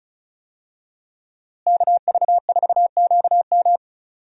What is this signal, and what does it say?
Morse code, heard as a keyed tone, about 34 wpm, 700 Hz. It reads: KV4QM
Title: USA Call Signs - 5 Characters - Repeated 2x - 35wpm